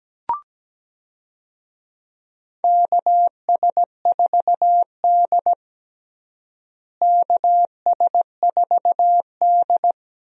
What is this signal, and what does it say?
Morse code, heard as a keyed tone, about 17 wpm, 700 Hz. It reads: KS4D KS4D
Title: USA Call Signs - 4 Characters - Repeated 3x - 17wpm